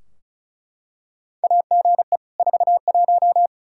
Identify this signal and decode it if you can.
Morse code, heard as a keyed tone, about 36 wpm, 700 Hz. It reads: AGE 41
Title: Practice QSO - Exchange 3 - More Info - 35wpm